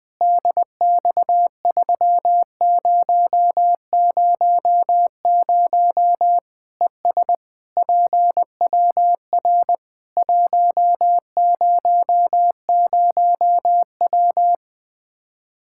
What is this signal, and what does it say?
Morse code, heard as a keyed tone, about 20 wpm, 700 Hz. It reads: DX3000 ES PWR 100W
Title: Practice QSO - Exchange 2 - Info - 20wpm